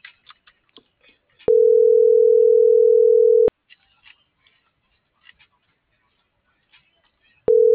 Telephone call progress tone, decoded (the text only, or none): ringback tone